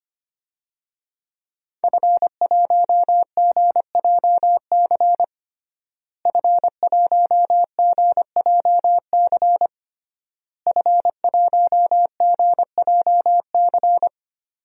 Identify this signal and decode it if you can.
Morse code, heard as a keyed tone, about 25 wpm, 700 Hz. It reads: F1GJC F1GJC F1GJC
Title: DXCC-100 Call Signs - 5 Characters - Repeated 3x - 25wpm